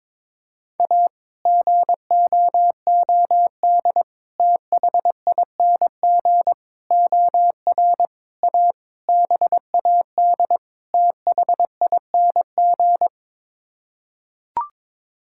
Morse code, heard as a keyed tone, about 22 wpm, 700 Hz. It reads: A GOOD THING OR A BAD THING
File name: practice / 22wpm / Sets of 7 Words - 22wpm